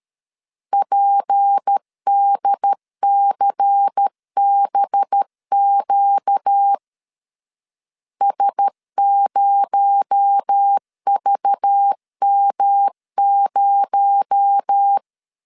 Morse code, heard as a keyed tone, about 13 wpm, 800 Hz. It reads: PDCBQ S0VM0
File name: CW / DET / 1 6 1500C S B 12wpm 3431